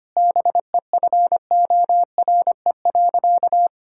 Morse code, heard as a keyed tone, about 25 wpm, 700 Hz. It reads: BEFORE.
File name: practice / 25wpm / Sentences with 3-Word Phrases - 25wpm